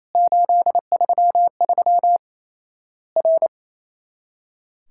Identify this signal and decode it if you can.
Morse code, heard as a keyed tone, about 28 wpm, 700 Hz. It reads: 833 R
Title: Worked All Europe DX Contest - QTC Exchange - 28wpm